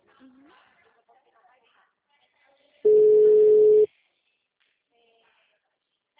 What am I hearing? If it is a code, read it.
Ringback tone (Europe, 425 Hz), with speech.